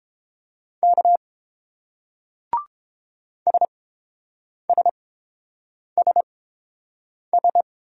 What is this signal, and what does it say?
Morse code, heard as a keyed tone, about 30 wpm, spaced as for 7 wpm, 700 Hz. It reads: K SSSS